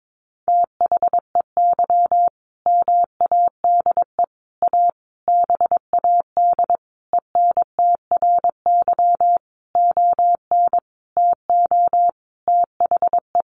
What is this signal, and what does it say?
Morse code, heard as a keyed tone, about 22 wpm, 700 Hz. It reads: THEY MADE A BAD ENTRY ON TO THE
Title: Sentences from Top 1500 Words 22wpm